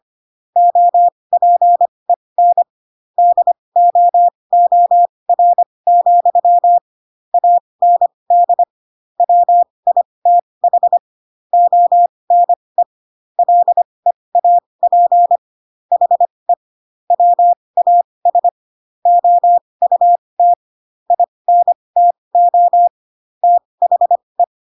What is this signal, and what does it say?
Morse code, heard as a keyed tone, about 25 wpm, spaced as for 20 wpm, 700 Hz. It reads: OPEN DOOR, AND WITH ONE LEAP HE WAS OUT INTO THE